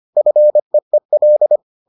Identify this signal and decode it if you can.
Morse code, heard as a keyed tone, about 25 wpm, 600 Hz. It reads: FEEL